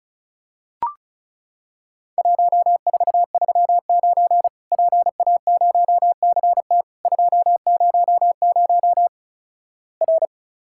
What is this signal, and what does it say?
Morse code, heard as a keyed tone, about 36 wpm, 700 Hz. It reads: 1439 PA0CT 200 R
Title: Worked All Europe DX Contest - QTC Exchange - 35wpm